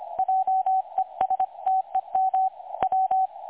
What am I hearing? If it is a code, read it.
Morse code, heard as a keyed tone, about 25 wpm, 760 Hz. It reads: JESTEM W
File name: dlazaawansowanych / nagraniazpasma / websdr recording 3565.0kHz